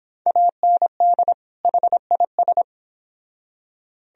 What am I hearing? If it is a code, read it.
Morse code, heard as a keyed tone, about 25 wpm, 700 Hz. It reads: AND HIS